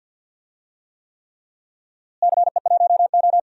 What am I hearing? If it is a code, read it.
Morse code, heard as a keyed tone, about 48 wpm, 700 Hz. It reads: CE1O
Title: DXCC-100 Call Signs - 4 Characters - Repeated 3x - 50wpm